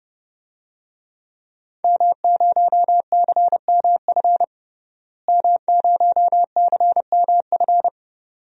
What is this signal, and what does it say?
Morse code, heard as a keyed tone, about 30 wpm, 700 Hz. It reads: M0CMF M0CMF
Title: DXCC-100 Call Signs - 5 Characters - Repeated 2x - 30wpm